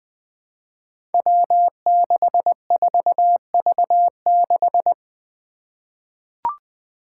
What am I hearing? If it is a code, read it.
Morse code, heard as a keyed tone, about 20 wpm, 700 Hz. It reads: W64V6
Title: Five Challenging Letter-Number Combinations 20wpm